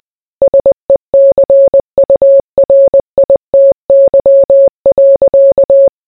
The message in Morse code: SECURITY.